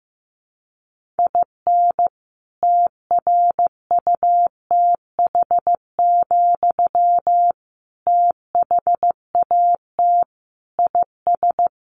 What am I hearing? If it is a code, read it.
Morse code, heard as a keyed tone, about 15 wpm, 700 Hz. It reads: IN TRUTH, THAT IS
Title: Sentences from Top 1300 Words 15wpm